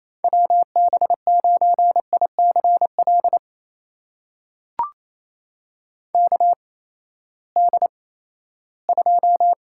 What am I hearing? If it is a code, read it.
Morse code, heard as a keyed tone, about 28 wpm, 700 Hz. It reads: W B 9 I C L K D 2
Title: USA Call Signs - 6 Characters - 8x Character Spacing - 28wpm